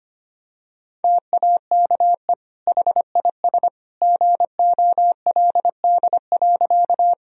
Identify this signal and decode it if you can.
Morse code, heard as a keyed tone, about 25 wpm, 700 Hz. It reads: TAKE HIS GOLD.
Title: Sentences from Top 1000 Words - Review Part 3 - 25wpm